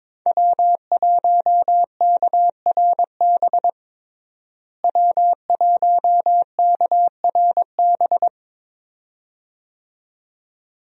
Morse code, heard as a keyed tone, about 22 wpm, 700 Hz. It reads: W1KRB W1KRB
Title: USA Call Signs - 5 Characters - Repeated 2x - 22wpm